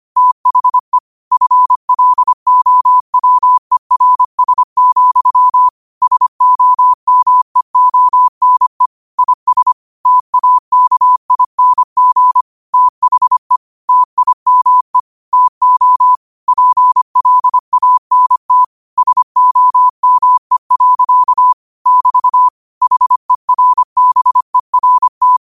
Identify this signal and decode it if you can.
Morse code, heard as a keyed tone, about 25 wpm, 1000 Hz. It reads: THE FLOWERS, SOMEONE IS TAKING THE TIME TO PLANT SOME. <BT> HERBERT
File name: 25wpm